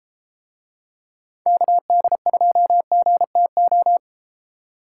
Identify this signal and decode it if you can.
Morse code, heard as a keyed tone, about 32 wpm, 700 Hz. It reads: KD2GTO